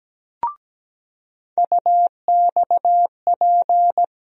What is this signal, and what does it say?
Morse code, heard as a keyed tone, about 17 wpm, 700 Hz. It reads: UXP
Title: USA Call Sign Postfixes - 3 Letters 17wpm